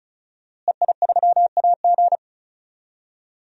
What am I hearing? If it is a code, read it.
Morse code, heard as a keyed tone, about 36 wpm, 700 Hz. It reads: EI3AG
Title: DXCC-100 Call Signs - 5 Characters - Repeated 3x - 35wpm